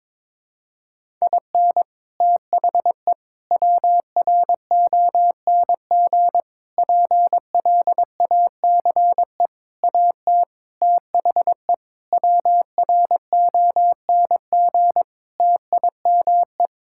Morse code, heard as a keyed tone, about 22 wpm, 700 Hz. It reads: IN THE WRONG PLACE AT THE WRONG TIME
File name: practice / 22wpm / Sets of 8 Words - 22wpm